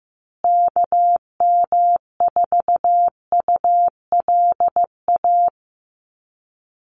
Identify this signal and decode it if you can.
Morse code, heard as a keyed tone, about 15 wpm, 700 Hz. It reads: KM4ULA
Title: USA Call Signs Repeated 2x 15wpm